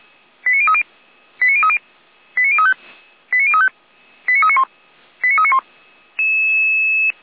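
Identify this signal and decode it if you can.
Five-tone selective call (ZVEI), heard as quick runs of five tones, tones of about 80 ms. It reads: 89030 89030 89035 89035 89391 89391
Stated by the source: ZVEI five-tone selective call received on an SDR.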